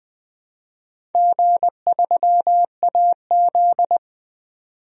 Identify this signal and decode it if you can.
Morse code, heard as a keyed tone, about 20 wpm, 700 Hz. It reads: G3AZ
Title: DXCC-100 Call Signs - 4 Characters - 20wpm